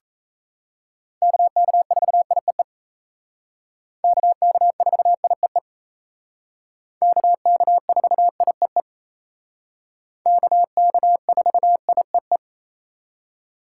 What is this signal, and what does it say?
Morse code, heard as a keyed tone, about 36 wpm, 700 Hz. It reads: KK4IEE KK4IEE KK4IEE KK4IEE